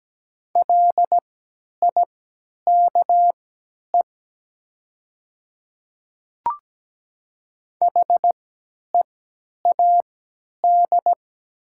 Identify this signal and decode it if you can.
Morse code, heard as a keyed tone, about 17 wpm, spaced as for 10 wpm, 700 Hz. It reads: LIKE HEAD
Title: 3x Character Spacing - Top 300 Words - 17wpm